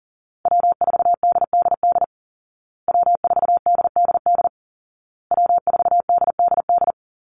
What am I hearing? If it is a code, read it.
Morse code, heard as a keyed tone, about 40 wpm, 700 Hz. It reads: W4DDD W4DDD W4DDD
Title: Call Signs Repeated 2x - 40wpm